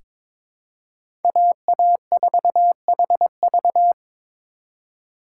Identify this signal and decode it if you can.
Morse code, heard as a keyed tone, about 22 wpm, 700 Hz. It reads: AA4HV
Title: K1USN SST QSO Practice - 22wpm